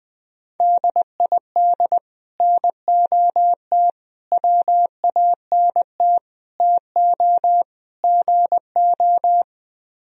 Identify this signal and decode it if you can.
Morse code, heard as a keyed tone, about 20 wpm, 700 Hz. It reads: DID NOT WANT TO GO